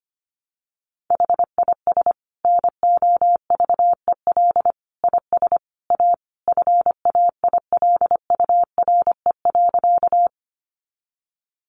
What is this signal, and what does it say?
Morse code, heard as a keyed tone, about 25 wpm, 700 Hz. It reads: HIS NOVEL IS A FAILURE.
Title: Sentences from Top 1300 Words 25wpm